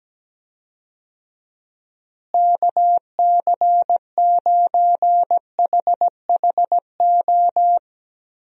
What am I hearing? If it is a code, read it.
Morse code, heard as a keyed tone, about 17 wpm, 700 Hz. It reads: KC9HHO
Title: USA Call Signs - 6 Characters - Repeated 2x - 17wpm